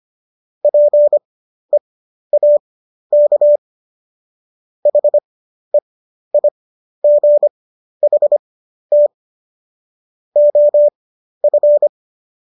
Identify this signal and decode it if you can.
Morse code, heard as a keyed tone, about 25 wpm, spaced as for 12 wpm, 590 Hz. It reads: PEAK HEIGHT OF